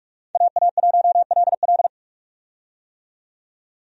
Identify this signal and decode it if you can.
Morse code, heard as a keyed tone, about 44 wpm, 700 Hz. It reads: AA1LL
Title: USA Call Signs - 5 Characters - 45wpm